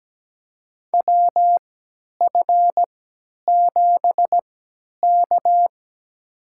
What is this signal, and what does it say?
Morse code, heard as a keyed tone, about 17 wpm, spaced as for 10 wpm, 700 Hz. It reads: WF7K
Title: USA Call Signs - 4 Characters - 3x Character Spacing - 17wpm